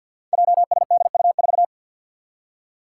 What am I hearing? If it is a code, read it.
Morse code, heard as a keyed tone, about 52 wpm, 700 Hz. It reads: JSDU4
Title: Five Challenging Letter-Number Combinations 50wpm